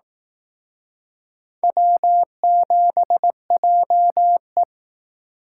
Morse code, heard as a keyed tone, about 18 wpm, 700 Hz. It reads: W7JE
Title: Speed Racing - USA 1x2 Call Signs 15wpm